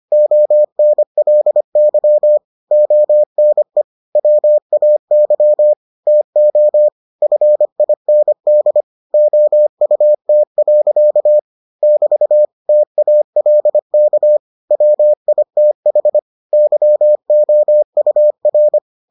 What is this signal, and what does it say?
Morse code, heard as a keyed tone, about 25 wpm, 600 Hz. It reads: ONLY ONE WAY TO FIND OUT. <BT> TALK WITH YOUR